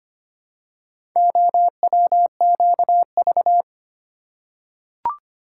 Morse code, heard as a keyed tone, about 25 wpm, 700 Hz. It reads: OWQV
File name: practice / 25wpm / Four Letter-Number Combinations - 25wpm